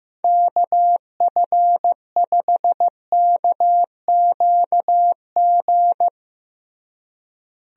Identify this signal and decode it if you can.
Morse code, heard as a keyed tone, about 16 wpm, 700 Hz. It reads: KF5KQG
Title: USA Call Signs - 6 Characters - Repeated 2x - 15wpm